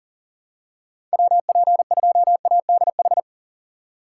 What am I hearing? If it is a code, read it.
Morse code, heard as a keyed tone, about 38 wpm, 700 Hz. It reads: WP2ADH
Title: USA Call Signs - 4x Character Spacing - 40wpm